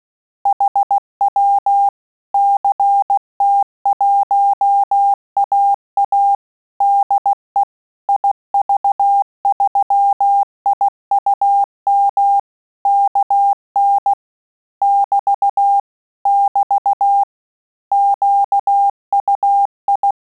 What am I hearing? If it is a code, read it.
Morse code, heard as a keyed tone, about 16 wpm, 800 Hz. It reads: HW CT1AA DE IV3IUM KN <BT> <BT> QUI
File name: Telegraphy-group / QSO CW 15 PPM